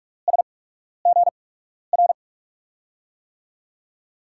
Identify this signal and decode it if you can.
Morse code, heard as a keyed tone, about 46 wpm, spaced as for 12 wpm, 700 Hz. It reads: SGR
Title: USA Call Signs - 6 Characters - 8x Character Spacing - 45wpm